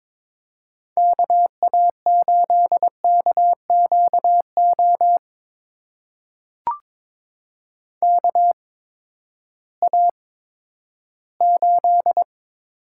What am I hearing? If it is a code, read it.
Morse code, heard as a keyed tone, about 22 wpm, spaced as for 13 wpm, 700 Hz. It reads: KA8KQO KA8
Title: USA Call Signs - 6 Characters - 8x Character Spacing - 22wpm